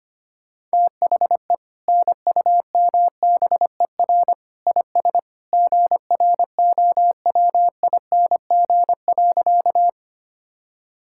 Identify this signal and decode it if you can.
Morse code, heard as a keyed tone, about 25 wpm, 700 Hz. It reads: THE NUMBER IS GROWING.
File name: practice / 25wpm / Sentences with 2-Word Phrases - 25wpm